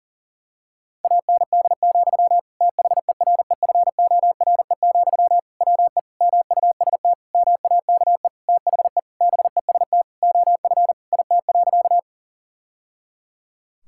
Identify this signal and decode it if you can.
Morse code, heard as a keyed tone, about 37 wpm, 700 Hz. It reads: AND, THEREFORE, WE MUST MAKE THE BEST OF IT.
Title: Sentences from Top 1000 Words - Review Part 5 - 40wpm